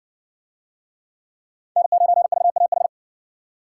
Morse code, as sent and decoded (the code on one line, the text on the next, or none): -. ----. ...- .- ....
N9VAH